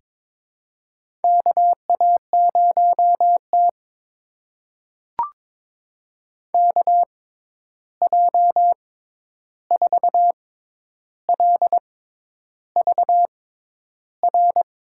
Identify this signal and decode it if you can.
Morse code, heard as a keyed tone, about 22 wpm, spaced as for 10 wpm, 700 Hz. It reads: KA0T KJ4LVR
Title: USA Call Signs - 6x Character Spacing - 22wpm